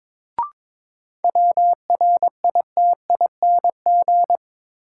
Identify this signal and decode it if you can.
Morse code, heard as a keyed tone, about 21 wpm, 700 Hz. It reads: WRITING
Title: Sentences from Top 800 Words - New Word Drill - 22wpm